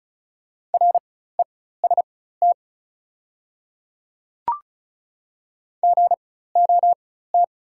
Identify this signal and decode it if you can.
Morse code, heard as a keyed tone, about 33 wpm, spaced as for 16 wpm, 690 Hz. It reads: REST GOT
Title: 4x Character Spacing - Top 500 Words - 35wpm